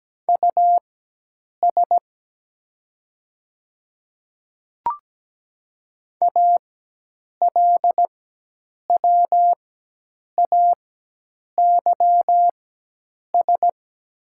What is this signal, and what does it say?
Morse code, heard as a keyed tone, about 17 wpm, spaced as for 8 wpm, 700 Hz. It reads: US ALWAYS
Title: 4x Character Spacing - Top 300 Words - 17wpm